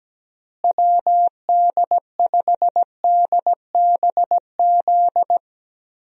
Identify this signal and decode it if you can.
Morse code, heard as a keyed tone, about 17 wpm, 700 Hz. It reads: WD5DBZ
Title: USA Call Signs - 6 Characters - Repeated 2x - 17wpm